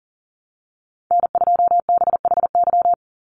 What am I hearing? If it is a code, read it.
Morse code, heard as a keyed tone, about 40 wpm, 700 Hz. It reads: N2BHY
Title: USA Call Signs 40wpm